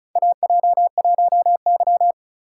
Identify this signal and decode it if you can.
Morse code, heard as a keyed tone, about 36 wpm, 690 Hz. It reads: AJ1Y
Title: USA Call Signs - 4 Characters - 2x Character Spacing - 35wpm